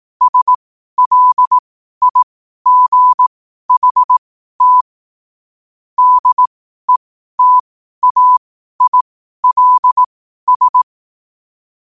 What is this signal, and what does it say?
Morse code, heard as a keyed tone, about 18 wpm, spaced as for 12 wpm, 1000 Hz. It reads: SLIGHT DETAILS
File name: qotdfiles / 12wpm